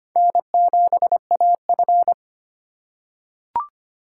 Morse code, heard as a keyed tone, about 25 wpm, 690 Hz. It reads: N7AF
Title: USA Call Signs - 4 Characters - 4x Character Spacing - 25wpm